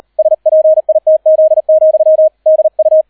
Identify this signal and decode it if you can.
Morse code, heard as a keyed tone, about 38 wpm, 630 Hz, with radio noise. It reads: SPITZ, DU